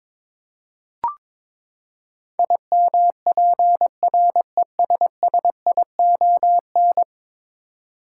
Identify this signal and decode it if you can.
Morse code, heard as a keyed tone, about 22 wpm, 700 Hz. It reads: IMPRESSION
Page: morse-code-ninja-practice-22wpm